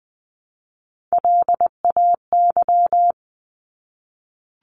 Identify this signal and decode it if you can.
Morse code, heard as a keyed tone, about 20 wpm, 700 Hz. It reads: LAY
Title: Top 500 Words 20wpm